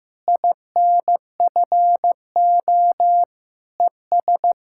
Morse code, heard as a keyed tone, about 15 wpm, 700 Hz. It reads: INFO ES
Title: Practice QSO - Exchange 4 - Ending - 15wpm